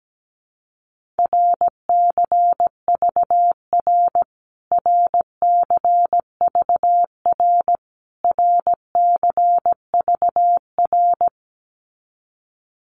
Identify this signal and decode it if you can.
Morse code, heard as a keyed tone, about 17 wpm, 700 Hz. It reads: RCVR RCVR RCVR
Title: Abbreviations Repeated 3x - 17wpm